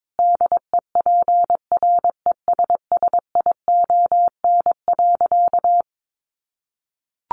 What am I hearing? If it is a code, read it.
Morse code, heard as a keyed tone, about 22 wpm, 700 Hz. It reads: DEPRESSION.
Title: Sentences from Top 1300 Words 22wpm